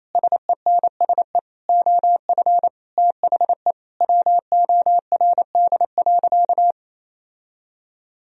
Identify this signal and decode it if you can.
Morse code, heard as a keyed tone, about 28 wpm, 700 Hz. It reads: SENSE OF THE WORD.
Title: Sentences from Top 1000 Words - Review Part 4 - 28wpm